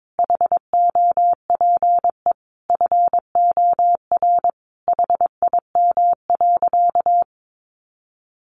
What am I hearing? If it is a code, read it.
Morse code, heard as a keyed tone, about 22 wpm, 700 Hz. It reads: HOPE FOR HIM.
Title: Sentences from Top 1000 Words 22wpm